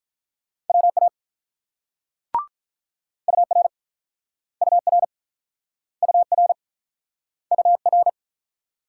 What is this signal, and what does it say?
Morse code, heard as a keyed tone, about 48 wpm, 700 Hz. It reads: WA UR UR UR UR